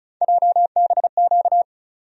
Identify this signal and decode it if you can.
Morse code, heard as a keyed tone, about 35 wpm, 700 Hz. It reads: JBQ
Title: USA Call Sign Postfixes - 3 Letters 35wpm